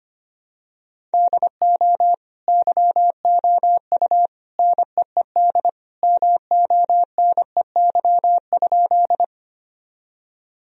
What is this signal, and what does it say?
Morse code, heard as a keyed tone, about 25 wpm, 700 Hz. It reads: DO YOU NEED MONEY?
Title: Sentences from Top 1000 Words - Review Part 1 - 25wpm